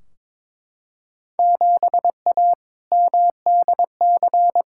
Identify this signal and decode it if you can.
Morse code, heard as a keyed tone, about 21 wpm, 700 Hz. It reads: 7A MDC
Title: Field Day Exchanges Encore - 22wpm